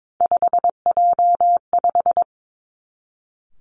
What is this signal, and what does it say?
Morse code, heard as a keyed tone, about 22 wpm, 700 Hz. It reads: 5J5
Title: Three Challenging Letter-Number Combinations 22wpm